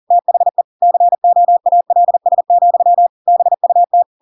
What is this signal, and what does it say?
Morse code, heard as a keyed tone, about 40 wpm, 700 Hz. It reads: THE COALS, BUT
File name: Pinocchio / Pinocchio ch7 40 40 ik1hge